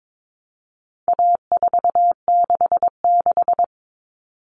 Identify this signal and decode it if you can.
Morse code, heard as a keyed tone, about 22 wpm, 700 Hz. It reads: A466
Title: Common Cut Numbers - 4-Digit 22wpm